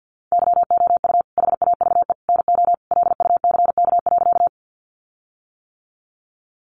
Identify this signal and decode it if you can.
Morse code, heard as a keyed tone, about 50 wpm, 700 Hz. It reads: YOU HAVE NO LUCK.
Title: Sentences from Top 1500 Words 50wpm